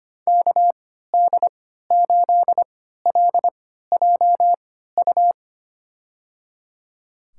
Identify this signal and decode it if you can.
Morse code, heard as a keyed tone, about 25 wpm, spaced as for 14 wpm, 700 Hz. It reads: KD8LJU
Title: USA Call Signs - 6 Characters - 3x Character Spacing - 25wpm